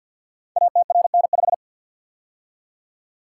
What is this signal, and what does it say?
Morse code, heard as a keyed tone, about 48 wpm, 700 Hz. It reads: ATRN5
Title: Code Groups - 5 Letter:Numbers - 50wpm